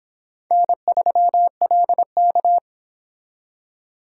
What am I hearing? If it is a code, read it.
Morse code, heard as a keyed tone, about 26 wpm, 700 Hz. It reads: N3LK